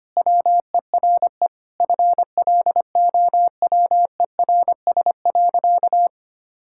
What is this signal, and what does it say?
Morse code, heard as a keyed tone, about 25 wpm, 700 Hz. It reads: WERE FLOWERS.
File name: practice / 25wpm / Sentences with 9-Word Phrases - 25wpm